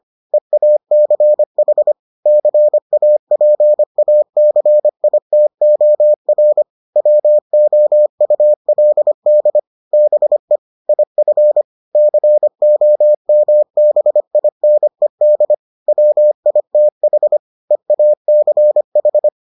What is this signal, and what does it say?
Morse code, heard as a keyed tone, about 25 wpm, 600 Hz. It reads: EACH CAPACITOR WOULD BE IF COMBINED WITH EACH